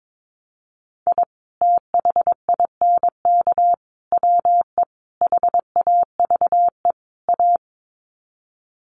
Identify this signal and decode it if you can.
Morse code, heard as a keyed tone, about 22 wpm, 700 Hz. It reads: I THINK WE HAVE A
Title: Sets of 5 Words Encore 22wpm